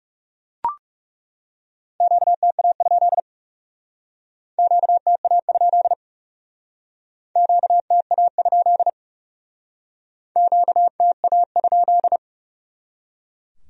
Morse code, heard as a keyed tone, about 37 wpm, 700 Hz. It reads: QTA? QTA? QTA? QTA?
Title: Speed-Racing Q-Codes - 30wpm